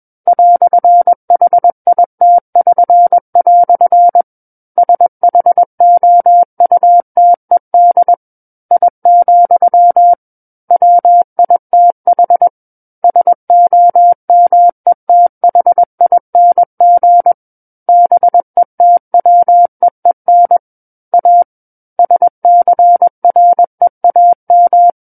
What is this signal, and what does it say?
Morse code, heard as a keyed tone, about 21 wpm, 700 Hz. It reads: "HIT<SN>" SHOUTED I, WITH SOMETHING BETWEEN A SCREAM
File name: Book One / 12-Chapter 12 - 21 WPM